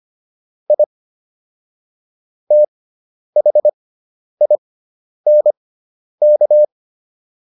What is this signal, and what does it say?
Morse code, heard as a keyed tone, about 25 wpm, spaced as for 10 wpm, 600 Hz. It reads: I THINK